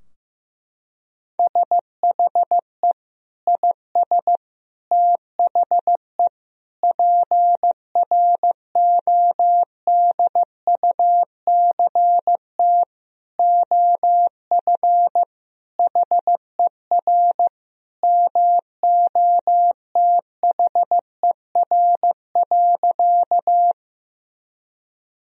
Morse code, heard as a keyed tone, about 15 wpm, 700 Hz. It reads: SHE IS THE PRODUCT OF HER MOTHER.
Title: Sentences from Top 1000 Words - Review Part 2 - 15wpm